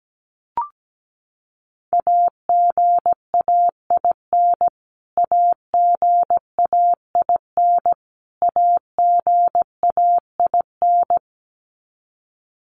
Morse code, heard as a keyed tone, about 17 wpm, 700 Hz. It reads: AGAIN AGAIN AGAIN
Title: Top 300 Words Repeated 17wpm